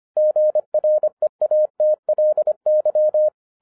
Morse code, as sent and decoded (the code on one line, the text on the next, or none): --. .-. .- - .-.. -.--
GREATLY